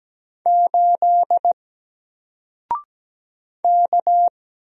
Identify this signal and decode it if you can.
Morse code, heard as a keyed tone, about 17 wpm, 700 Hz. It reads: 8 K
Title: Single Letter-Number Alt 17wpm